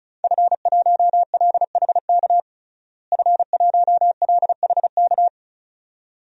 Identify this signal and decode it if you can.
Morse code, heard as a keyed tone, about 34 wpm, 700 Hz. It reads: F1LHK F1LHK
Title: DXCC-100 Call Signs - 5 Characters - Repeated 2x - 35wpm